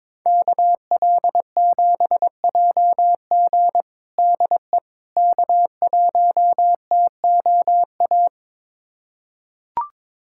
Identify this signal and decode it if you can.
Morse code, heard as a keyed tone, about 22 wpm, 700 Hz. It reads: KL7JG DE K1TOA
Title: USA Callsign DE Callsign - 5 Characters - 22wpm